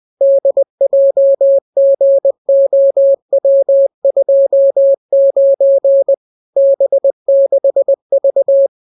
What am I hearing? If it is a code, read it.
Morse code, heard as a keyed tone, about 20 wpm, 550 Hz. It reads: DJGOW29 B6V